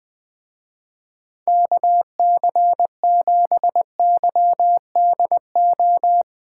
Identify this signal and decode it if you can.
Morse code, heard as a keyed tone, about 20 wpm, 700 Hz. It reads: KC7YDO
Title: Speed Racing - USA 2x3 Call Signs 20wpm